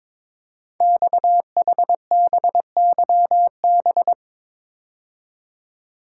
Morse code, heard as a keyed tone, about 22 wpm, 700 Hz. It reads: XHBYB
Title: Code Groups - 5 Letters - 22wpm